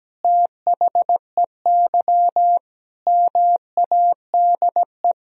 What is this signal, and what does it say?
Morse code, heard as a keyed tone, about 17 wpm, 700 Hz. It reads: THEY MADE
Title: Sentences from Top 200 Words and Abbreviations 17wpm